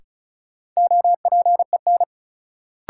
Morse code, heard as a keyed tone, about 35 wpm, 700 Hz. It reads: OPEN
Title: Sentences from Top 300 Words - New Word Drill - 35wpm